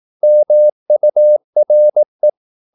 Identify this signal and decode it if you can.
Morse code, heard as a keyed tone, about 18 wpm, 600 Hz. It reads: MURE
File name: Korte ord DK 18wpm